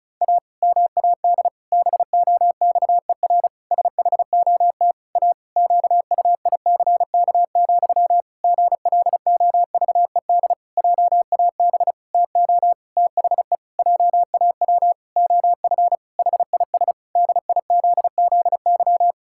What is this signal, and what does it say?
Morse code, heard as a keyed tone, about 34 wpm, 700 Hz. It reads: A MAD BOXER SHOT A QUICK, GLOVED JAB TO THE JAW OF HIS DIZZY
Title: Pangram Challenge v2 - 35wpm